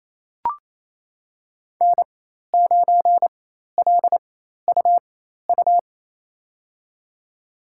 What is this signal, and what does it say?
Morse code, heard as a keyed tone, about 28 wpm, spaced as for 13 wpm, 700 Hz. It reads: N9LUU
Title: USA Call Signs - 5 Characters - 4x Character Spacing - 28wpm